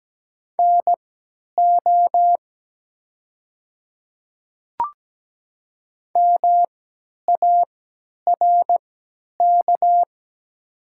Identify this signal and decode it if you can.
Morse code, heard as a keyed tone, about 17 wpm, spaced as for 10 wpm, 700 Hz. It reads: NO MARK
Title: 3x Character Spacing - Top 300 Words - 17wpm